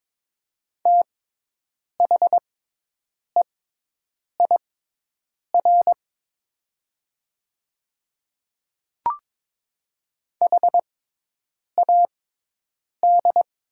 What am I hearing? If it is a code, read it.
Morse code, heard as a keyed tone, about 22 wpm, spaced as for 8 wpm, 700 Hz. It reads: THEIR HAD
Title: 6x Character Spacing - Top 200 Words - 22wpm